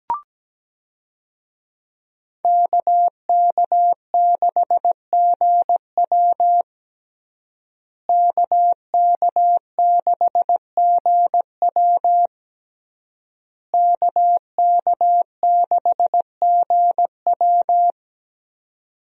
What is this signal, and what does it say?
Morse code, heard as a keyed tone, about 17 wpm, 700 Hz. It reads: KK6GW KK6GW KK6GW
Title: USA Call Signs - 5 Characters - Repeated 3x - 17wpm